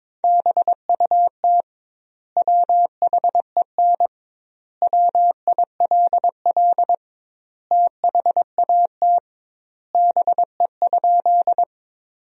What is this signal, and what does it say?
Morse code, heard as a keyed tone, about 21 wpm, 700 Hz. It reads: BUT WHEN WILL THAT BE?
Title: 2x Word-Spacing - Sentences from Top 100 Words - 22wpm